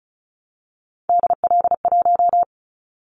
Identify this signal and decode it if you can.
Morse code, heard as a keyed tone, about 36 wpm, 700 Hz. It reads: DL1
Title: Callsign Prefixes for Top 50 DXCC - 35wpm